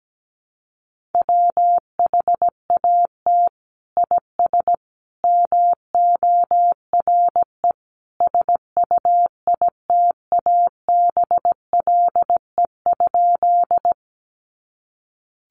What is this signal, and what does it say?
Morse code, heard as a keyed tone, about 17 wpm, 700 Hz. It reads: WHAT IS MORE SUITABLE?
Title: Sentences from Top 1400 Words 17wpm